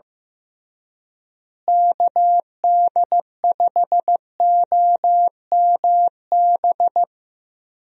Morse code, heard as a keyed tone, about 15 wpm, 700 Hz. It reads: KD5OMB
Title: USA Call Signs - 6 Characters - Repeated 3x - 15wpm